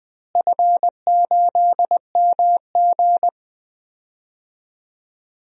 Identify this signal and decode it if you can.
Morse code, heard as a keyed tone, about 20 wpm, 700 Hz. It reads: F8MG
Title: DXCC-100 Call Signs - 4 Characters - Repeated 3x - 20wpm